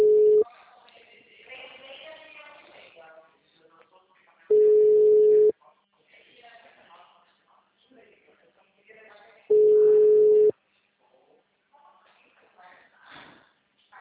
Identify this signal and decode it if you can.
Ringback tone (Europe, 425 Hz), with speech.
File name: out- -2022- - - .19570